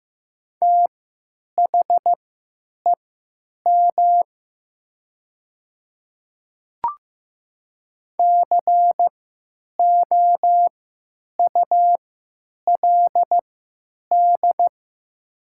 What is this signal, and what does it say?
Morse code, heard as a keyed tone, about 15 wpm, spaced as for 9 wpm, 700 Hz. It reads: THEM COULD
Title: 3x Character Spacing - Top 200 Words - 15wpm